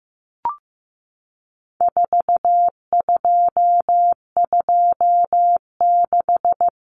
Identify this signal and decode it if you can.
Morse code, heard as a keyed tone, about 15 wpm, 700 Hz. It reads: 4226
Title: Common Cut Numbers - 4-Digit 15wpm